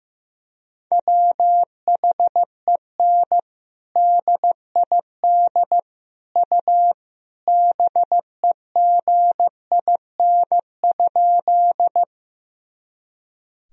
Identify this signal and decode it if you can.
Morse code, heard as a keyed tone, about 16 wpm, 700 Hz. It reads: WHEN DID U BEGIN?
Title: Sentences from Top 300 Words and Abbreviations 15wpm